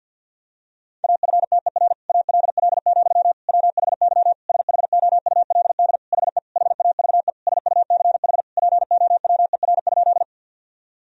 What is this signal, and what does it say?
Morse code, heard as a keyed tone, about 51 wpm, 700 Hz. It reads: AFTER ALL, WHY SHOULD HE HAVE SUCH POWER?
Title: Sentences from Top 1000 Words - Review Part 2 - 50wpm